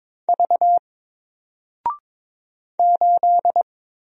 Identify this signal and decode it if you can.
Morse code, heard as a keyed tone, about 22 wpm, 700 Hz. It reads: V 8
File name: Single-Letter-Number-22wpm